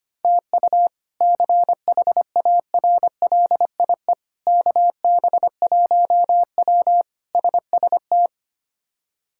Morse code, heard as a keyed tone, about 25 wpm, 700 Hz. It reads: TU CHARLIE KB1W SST
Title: K1USN SST QSO Practice - 25wpm